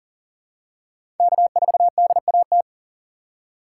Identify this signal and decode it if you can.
Morse code, heard as a keyed tone, about 40 wpm, 700 Hz. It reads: K4DAT
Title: USA Call Signs - 8x Character Spacing - 40wpm